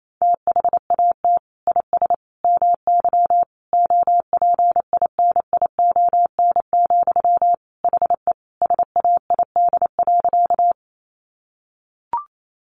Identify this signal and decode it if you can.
Morse code, heard as a keyed tone, about 28 wpm, 700 Hz. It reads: THAT IS MY OPINION, HE SAID.
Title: Sentences from Top 1200 Words 28wpm